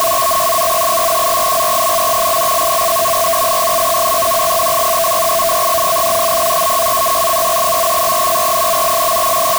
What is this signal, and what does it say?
Two-tone teleprinter signal, about 420 Hz shift.